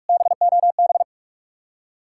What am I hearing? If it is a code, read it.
Morse code, heard as a keyed tone, about 58 wpm, 700 Hz. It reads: BOB